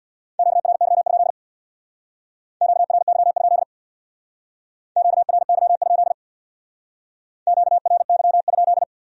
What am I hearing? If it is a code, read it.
Morse code, heard as a keyed tone, about 65 wpm, 700 Hz. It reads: QRY? QRY? QRY? QRY?